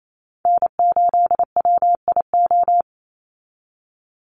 Morse code, heard as a keyed tone, about 28 wpm, 700 Hz. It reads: N8WIO
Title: USA Call Signs 28wpm